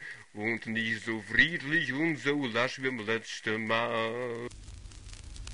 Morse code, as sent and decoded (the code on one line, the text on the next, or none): none